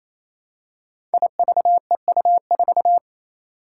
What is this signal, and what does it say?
Morse code, heard as a keyed tone, about 28 wpm, 700 Hz. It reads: IVEU4
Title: Five Challenging Letter-Number Combinations 28wpm